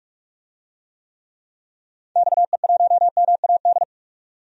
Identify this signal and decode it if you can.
Morse code, heard as a keyed tone, about 47 wpm, 700 Hz. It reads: XE1MAD
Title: DXCC-100 Call Signs - 6 Characters - 45wpm